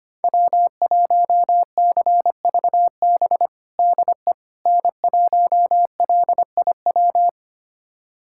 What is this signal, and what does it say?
Morse code, heard as a keyed tone, about 25 wpm, 700 Hz. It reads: W1CVB DE N1LIW
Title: USA Callsign DE Callsign - 5 Characters - 25wpm